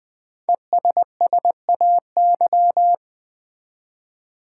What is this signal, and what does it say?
Morse code, heard as a keyed tone, about 20 wpm, 700 Hz. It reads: ESSAY
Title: Five Letter Words - 20wpm